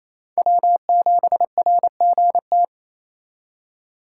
Morse code, heard as a keyed tone, about 33 wpm, 700 Hz. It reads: W7RGT
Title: Speed Racing - USA 1x3 Call Signs 28wpm